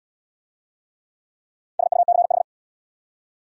Morse code, heard as a keyed tone, about 66 wpm, 700 Hz. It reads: SUCH